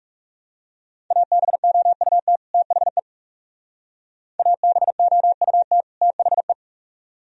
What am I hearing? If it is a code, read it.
Morse code, heard as a keyed tone, about 39 wpm, 700 Hz. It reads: ABOUT THE ABOUT THE